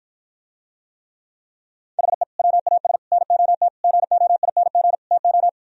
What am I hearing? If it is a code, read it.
Morse code, heard as a keyed tone, about 53 wpm, 700 Hz. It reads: HE WAS NOT GOING TO